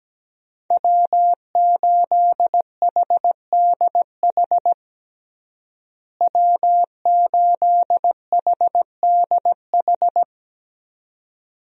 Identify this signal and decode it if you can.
Morse code, heard as a keyed tone, about 17 wpm, 700 Hz. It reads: W8HDH W8HDH
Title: USA Call Signs - 5 Characters - Repeated 2x - 17wpm